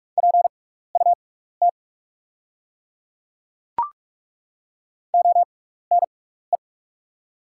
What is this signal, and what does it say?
Morse code, heard as a keyed tone, about 48 wpm, spaced as for 16 wpm, 700 Hz. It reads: PUT ONE
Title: 6x Character Spacing - Top 200 Words - 45wpm